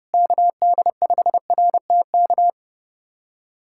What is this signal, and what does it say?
Morse code, heard as a keyed tone, about 32 wpm, 700 Hz. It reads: KD5RTK
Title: USA Callsign DE Callsign - 6 Characters - 30wpm